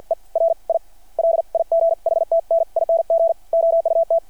Morse code, heard as a keyed tone, about 48 wpm, spaced as for 41 wpm, 660 Hz. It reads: ERI LIGHTNING OUT
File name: archived / CRSnet